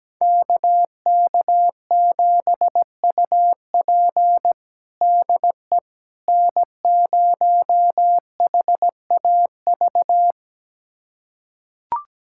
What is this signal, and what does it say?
Morse code, heard as a keyed tone, about 17 wpm, 700 Hz. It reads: KK7UP DE N0HAV
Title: USA Callsign DE Callsign - 5 Characters - 17wpm